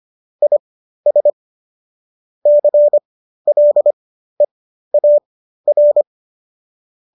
Morse code, heard as a keyed tone, about 25 wpm, spaced as for 13 wpm, 590 Hz. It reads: IS CLEAR